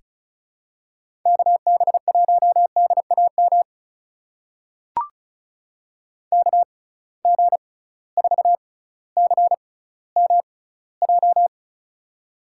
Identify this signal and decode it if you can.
Morse code, heard as a keyed tone, about 36 wpm, spaced as for 18 wpm, 700 Hz. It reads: KB1DAM KG4CMJ
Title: USA Call Signs - 6 Characters - 6x Character Spacing - 35wpm